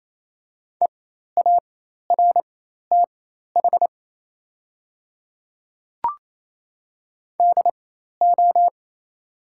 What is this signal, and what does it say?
Morse code, heard as a keyed tone, about 28 wpm, spaced as for 13 wpm, 690 Hz. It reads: EARTH DO
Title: 4x Character Spacing - Top 500 Words - 28wpm